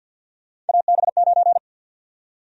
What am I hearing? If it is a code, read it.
Morse code, heard as a keyed tone, about 52 wpm, 700 Hz. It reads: AB9
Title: USA Call Sign Prefixes - Repeated 2x - 50wpm